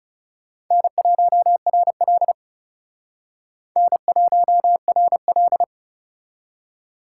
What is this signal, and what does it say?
Morse code, heard as a keyed tone, about 38 wpm, 700 Hz. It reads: N1RL N1RL